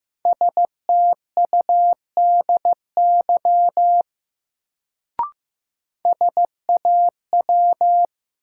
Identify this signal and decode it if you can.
Morse code, heard as a keyed tone, about 15 wpm, 700 Hz. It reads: STUDY SAW
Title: Sentences from Top 300 Words - New Word Drill - 15wpm